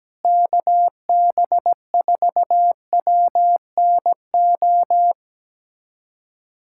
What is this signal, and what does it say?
Morse code, heard as a keyed tone, about 17 wpm, 700 Hz. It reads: KB4WNO
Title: USA Callsign DE Callsign 17wpm